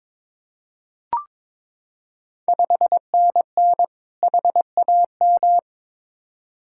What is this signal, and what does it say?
Morse code, heard as a keyed tone, about 22 wpm, 700 Hz. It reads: 5NN HAM
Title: Ontario QSO Party Exchange 22wpm